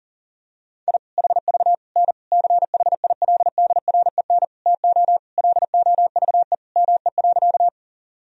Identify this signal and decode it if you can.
Morse code, heard as a keyed tone, about 40 wpm, 700 Hz. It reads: I HV N CHILDREN TO LOVE ME.
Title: Sentences from Top 400 Words and Abbreviations 40wpm